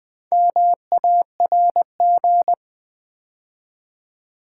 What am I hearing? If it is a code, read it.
Morse code, heard as a keyed tone, about 20 wpm, 700 Hz. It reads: MARG
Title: Build A Word - 2 Letters At A Time - 6-Letter Words - 20wpm